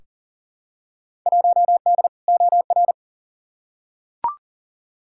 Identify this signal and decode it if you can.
Morse code, heard as a keyed tone, about 40 wpm, 700 Hz. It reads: 1D OR
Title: Field Day Exchanges - 40wpm